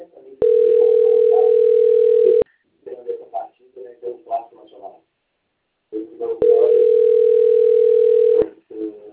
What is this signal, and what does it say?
Ringback tone (North America), 440 + 480 Hz.
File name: out- - - - - .59059